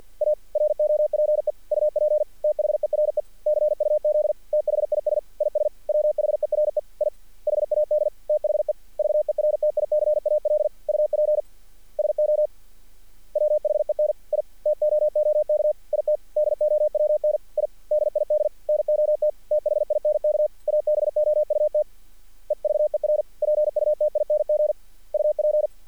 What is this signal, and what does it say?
Morse code, heard as a keyed tone, about 48 wpm, 600 Hz. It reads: A ROPE UP THERE CUZ THIS IS WHERE I HAD THE VERTICAL UP SO WHEN I TOOK IT DOWN I DID NOT THINK ABOUT EVER PUTING UP
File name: archived / CRSnet